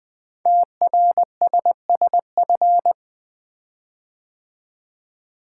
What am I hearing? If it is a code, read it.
Morse code, heard as a keyed tone, about 20 wpm, 700 Hz. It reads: TRSSF
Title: Code Groups - 5 Letters - 20wpm